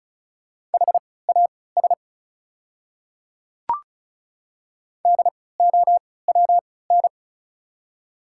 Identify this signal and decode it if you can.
Morse code, heard as a keyed tone, about 34 wpm, spaced as for 20 wpm, 700 Hz. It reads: HAS DOWN
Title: 3x Character Spacing - Top 100 Words - 35wpm